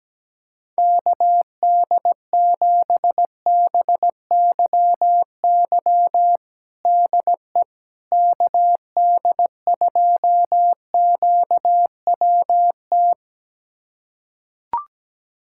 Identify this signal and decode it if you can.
Morse code, heard as a keyed tone, about 17 wpm, 700 Hz. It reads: KD7BYY DE KD2QWT
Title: USA Callsign DE Callsign - 6 Characters - 17wpm